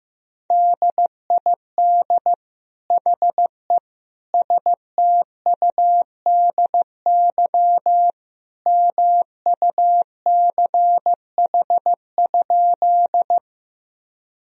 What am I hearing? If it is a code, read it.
Morse code, heard as a keyed tone, about 15 wpm, 700 Hz. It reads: DID HE STUDY MUCH?
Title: Sentences from Top 1000 Words - Review Part 1 - 15wpm